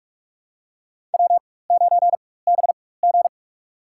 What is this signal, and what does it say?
Morse code, heard as a keyed tone, about 44 wpm, spaced as for 21 wpm, 700 Hz. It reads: W9BG
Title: USA Call Signs - 4x Character Spacing - 45wpm